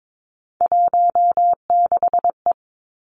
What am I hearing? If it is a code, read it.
Morse code, heard as a keyed tone, about 22 wpm, 700 Hz. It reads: 16E
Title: Three Challenging Letter-Number Combinations 22wpm